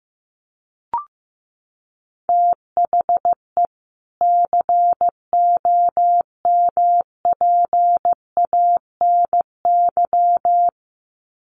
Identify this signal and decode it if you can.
Morse code, heard as a keyed tone, about 15 wpm, 700 Hz. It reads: THE COMPANY
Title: Sets of 2 Words 15wpm